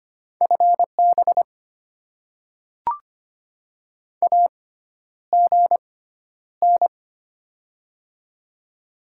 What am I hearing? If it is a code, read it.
Morse code, heard as a keyed tone, about 25 wpm, spaced as for 11 wpm, 700 Hz. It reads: FB AGN
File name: Top 20 QSO Elements - 6x Character Spacing - 25wpm